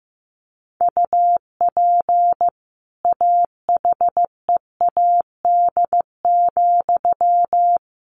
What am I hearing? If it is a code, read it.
Morse code, heard as a keyed tone, about 15 wpm, 700 Hz. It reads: UP AHEAD,
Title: A Sound Of Thunder 15wpm